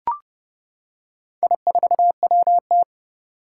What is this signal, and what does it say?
Morse code, heard as a keyed tone, about 30 wpm, 700 Hz. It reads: I4WT
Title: Four Challenging Letter-Number Combinations 30wpm